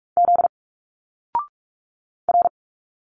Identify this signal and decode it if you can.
Morse code, heard as a keyed tone, about 46 wpm, 700 Hz. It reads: Z R
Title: Single Letters 45wpm